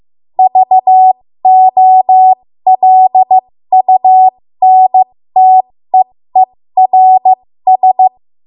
Morse code, heard as a keyed tone, about 15 wpm, spaced as for 13 wpm, 750 Hz. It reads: VOLUNTEERS
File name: Morse / 13WPM